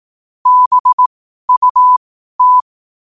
Morse code, heard as a keyed tone, about 18 wpm, spaced as for 12 wpm, 1000 Hz. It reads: BUT
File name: qotdfiles / 12wpm